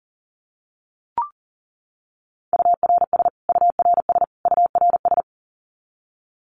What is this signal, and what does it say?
Morse code, heard as a keyed tone, about 40 wpm, 700 Hz. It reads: URS URS URS
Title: Abbreviations Repeated 3x - 40wpm